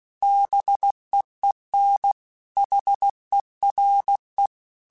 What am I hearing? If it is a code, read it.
Morse code, heard as a keyed tone, about 16 wpm, 800 Hz. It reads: BEEN HERE
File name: QSOs / QSOs-mp3s / WD0BGP 1@15